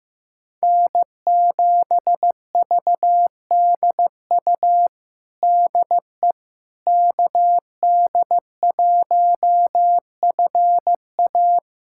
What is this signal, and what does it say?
Morse code, heard as a keyed tone, about 15 wpm, 700 Hz. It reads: N7VDU DE KD1FA
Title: USA Callsign DE Callsign - 5 Characters - 15wpm